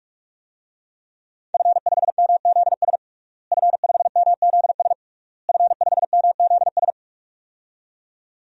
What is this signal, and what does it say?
Morse code, heard as a keyed tone, about 41 wpm, 700 Hz. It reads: F5MZS F5MZS F5MZS
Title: DXCC-100 Call Signs - 5 Characters - Repeated 3x - 45wpm